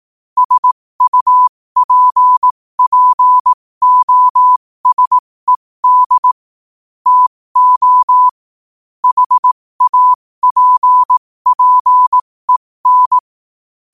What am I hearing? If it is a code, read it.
Morse code, heard as a keyed tone, about 18 wpm, spaced as for 15 wpm, 1000 Hz. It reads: SUPPOSED TO HAPPEN
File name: qotdfiles / 15wpm